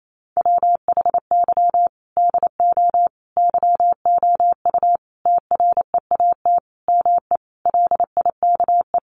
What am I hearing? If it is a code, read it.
Morse code, heard as a keyed tone, about 28 wpm, 700 Hz. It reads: WHY DO YOU TREAT ME LIKE
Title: Sentences from Top 1300 Words 28wpm